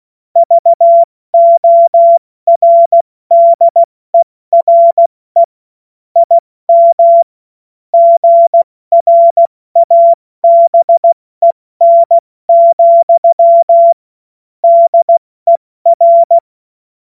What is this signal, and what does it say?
Morse code, heard as a keyed tone, about 16 wpm, 680 Hz. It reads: VORDERE IM GRABEN, DER
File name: Morsekurs / Witze / Witz 19 Dezember 2023 70 F80